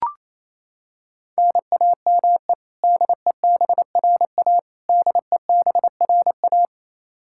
Morse code, heard as a keyed tone, about 27 wpm, 700 Hz. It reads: NAME DEBRA DEBRA
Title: Practice QSO - Exchange 1 - Beginning - 28wpm